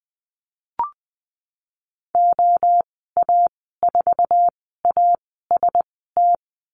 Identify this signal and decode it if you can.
Morse code, heard as a keyed tone, about 20 wpm, spaced as for 15 wpm, 700 Hz. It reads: OA4AST
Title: 122 Lesson 20g - Number 3 - Call Signs 20wpm